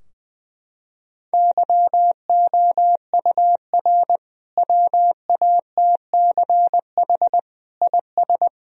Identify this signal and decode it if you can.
Morse code, heard as a keyed tone, about 21 wpm, 700 Hz. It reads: YOUR WATCH IS
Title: Sentences from Top 1000 Words - Review Part 2 - 20wpm